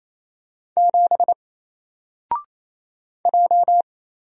Morse code, heard as a keyed tone, about 27 wpm, 700 Hz. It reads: 7 J
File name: Single-Letter-Number-28wpm